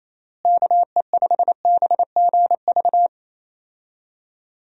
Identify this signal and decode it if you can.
Morse code, heard as a keyed tone, about 27 wpm, 700 Hz. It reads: KE5BGV
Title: USA Call Signs - 6 Characters - Repeated 2x - 28wpm